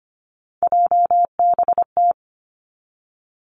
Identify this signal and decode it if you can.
Morse code, heard as a keyed tone, about 25 wpm, 700 Hz. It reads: JBT
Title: Three Challenging Letter-Number Combinations 25wpm